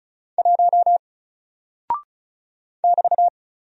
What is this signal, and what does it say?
Morse code, heard as a keyed tone, about 36 wpm, 700 Hz. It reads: 1 <BT>
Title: Single Character (Letter:Number:Punctuation:Prosign) 35wpm